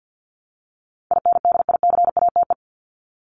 Morse code, heard as a keyed tone, about 50 wpm, 700 Hz. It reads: INDICATE
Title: ED Suffix - Top 100 Words with Base Words 50wpm